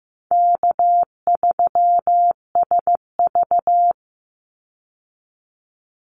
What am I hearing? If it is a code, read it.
Morse code, heard as a keyed tone, about 15 wpm, 700 Hz. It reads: K3SV
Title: Easy Call Signs Repeated 2x - 15wpm